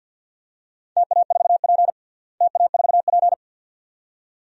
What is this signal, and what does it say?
Morse code, heard as a keyed tone, about 48 wpm, 700 Hz. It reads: TA4P TA4P
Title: DXCC-100 Call Signs - 4 Characters - Repeated 2x - 50wpm